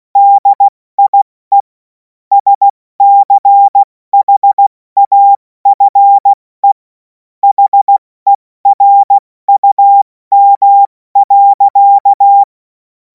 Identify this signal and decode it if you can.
Morse code, heard as a keyed tone, about 16 wpm, 800 Hz. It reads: DIE SCHAFE HERUM.
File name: Morsekurs / Witze / Witz 14. April 2026 70 F80